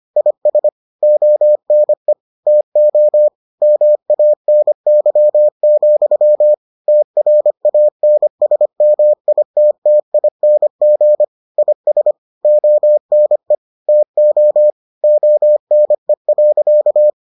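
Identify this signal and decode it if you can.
Morse code, heard as a keyed tone, about 25 wpm, 600 Hz. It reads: IS ONE TO MANY, TRANSMITTING IS ONE TO ONE.
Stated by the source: Morse edition of the Foundations of Amateur Radio episodes.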